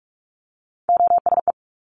Morse code, heard as a keyed tone, about 46 wpm, 700 Hz. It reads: OSE
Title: Top Three Letter Combinations in Words 45wpm